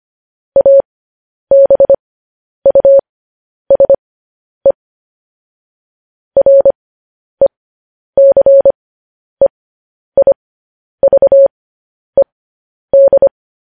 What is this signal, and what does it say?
Morse code, heard as a keyed tone, about 25 wpm, spaced as for 10 wpm, 550 Hz. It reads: ABUSE RECEIVED
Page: news-headlines-in-morse-code-at-10-wpm